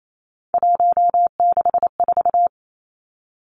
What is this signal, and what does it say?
Morse code, heard as a keyed tone, about 28 wpm, 700 Hz. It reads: 164
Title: Numbers - 3-Digit 28wpm